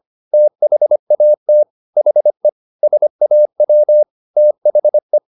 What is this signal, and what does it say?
Morse code, heard as a keyed tone, about 25 wpm, 600 Hz. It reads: THAT HE SAW THE